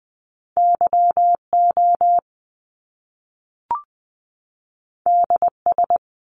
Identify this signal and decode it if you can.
Morse code, heard as a keyed tone, about 20 wpm, 700 Hz. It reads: Y O D S
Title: Top Two Letter Combinations in Words 20wpm